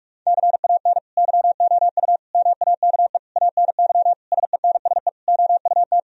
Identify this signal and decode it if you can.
Morse code, heard as a keyed tone, about 41 wpm, 700 Hz. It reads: CAN YOU MAKE ANY SENSE OUT
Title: Sentences from Top 1000 Words - Review Part 4 - 45wpm